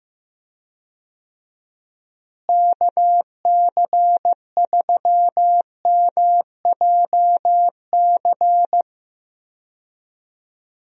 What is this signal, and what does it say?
Morse code, heard as a keyed tone, about 15 wpm, 700 Hz. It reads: KC3MJC
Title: USA Call Signs - 6 Characters - Repeated 3x - 15wpm